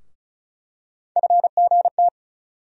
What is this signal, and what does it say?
Morse code, heard as a keyed tone, about 36 wpm, 700 Hz. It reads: FGT
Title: USA Call Sign Postfixes - 3 Letters 35wpm